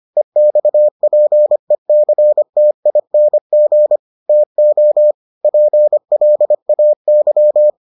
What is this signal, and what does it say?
Morse code, heard as a keyed tone, about 25 wpm, 600 Hz. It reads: EXPECTING TO PLAY